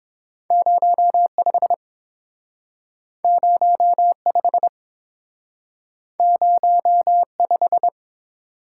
Morse code, heard as a keyed tone, about 30 wpm, 700 Hz. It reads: 0 5 0 5 0 5